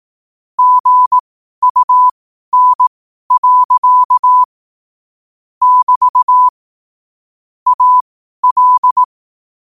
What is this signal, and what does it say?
Morse code, heard as a keyed tone, about 18 wpm, spaced as for 12 wpm, 1000 Hz. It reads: GUN. <BT> AL
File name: qotdfiles / 12wpm